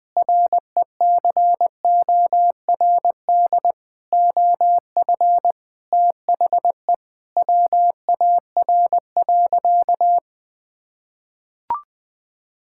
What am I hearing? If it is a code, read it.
Morse code, heard as a keyed tone, about 20 wpm, 700 Hz. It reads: RECORD OF THE WAR.
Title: Sentences from Top 1000 Words - Review Part 6 - 20wpm